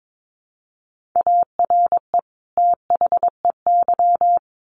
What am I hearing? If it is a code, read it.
Morse code, heard as a keyed tone, about 22 wpm, 700 Hz. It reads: ARE THEY
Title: Sentences from Top 1300 Words 22wpm